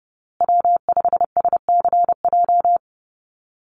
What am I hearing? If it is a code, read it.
Morse code, heard as a keyed tone, about 29 wpm, 700 Hz. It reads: W5SCJ
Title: USA Call Signs Repeated 2x 30wpm